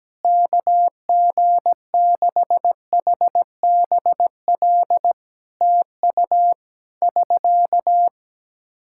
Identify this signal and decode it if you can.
Morse code, heard as a keyed tone, about 17 wpm, 700 Hz. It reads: KG6HBL TU <SK>
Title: Practice QSO - Exchange 4 - Ending - 17wpm